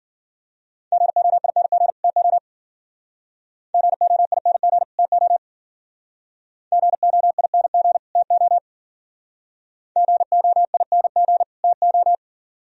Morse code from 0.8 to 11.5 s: GOING TO GOING TO GOING TO GOING

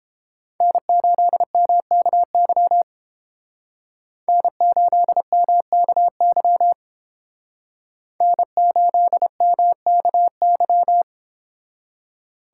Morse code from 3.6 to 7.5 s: N8MKY

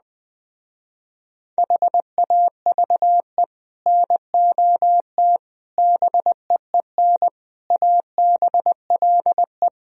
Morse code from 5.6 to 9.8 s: BEEN ABLE

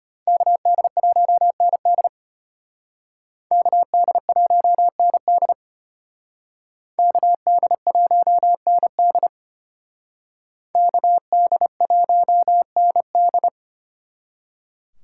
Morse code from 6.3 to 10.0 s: KD1ND

